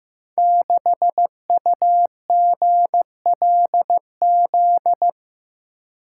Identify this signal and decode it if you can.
Morse code, heard as a keyed tone, about 15 wpm, 700 Hz. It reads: 6UGLZ
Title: Code Groups - 5 Letter:Numbers - 15wpm